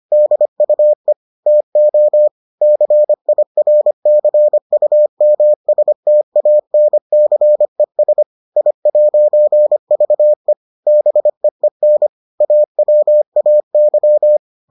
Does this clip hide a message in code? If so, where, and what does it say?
Yes: Morse code from 0.1 to 14.4 s, heard as a keyed tone, about 25 wpm, 600 Hz. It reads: DUE TO CIRCUMSTANCES I'VE BEEN AWAY